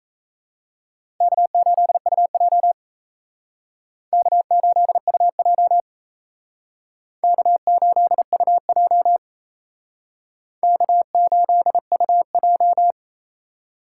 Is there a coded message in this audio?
Yes: Morse code from 1.2 to 12.9 s, heard as a keyed tone, about 36 wpm, 700 Hz. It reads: K8UJ K8UJ K8UJ K8UJ